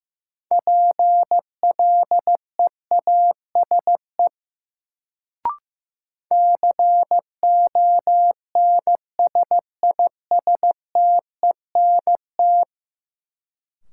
Morse code, heard as a keyed tone, about 15 wpm, 700 Hz. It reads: PLEASE CONSISTENT
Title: Sentences from Top 1500 Words - New Word Drill - 15wpm